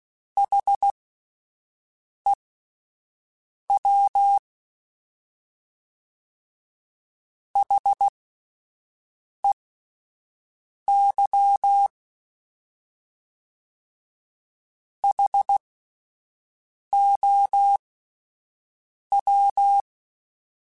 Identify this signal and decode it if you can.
Morse code, heard as a keyed tone, about 16 wpm, spaced as for 5 wpm, 800 Hz. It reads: HEW HEY HOW